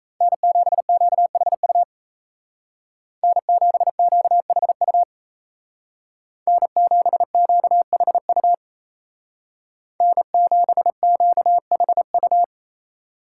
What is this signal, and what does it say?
Morse code, heard as a keyed tone, about 36 wpm, 700 Hz. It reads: N7QHU N7QHU N7QHU N7QHU